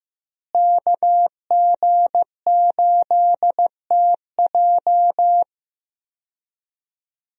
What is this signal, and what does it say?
Morse code, heard as a keyed tone, about 15 wpm, 700 Hz. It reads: KG8TJ
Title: USA Call Signs - 5 Characters - Repeated 2x - 15wpm